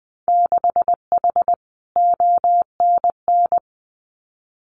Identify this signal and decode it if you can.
Morse code, heard as a keyed tone, about 20 wpm, 700 Hz. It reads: 6H ONN
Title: Winter Field Day Exchanges Encore 20wpm